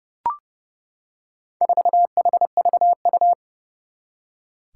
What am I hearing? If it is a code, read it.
Morse code, heard as a keyed tone, about 30 wpm, 700 Hz. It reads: 4HVU
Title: Four Challenging Letter-Number Combinations 30wpm